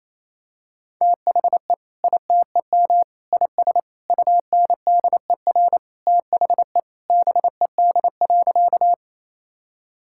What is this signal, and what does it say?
Morse code, heard as a keyed tone, about 28 wpm, 700 Hz. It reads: THE ITEM IS UNDER THE BED.
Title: Sentences from Top 1000 Words - Review Part 6 - 28wpm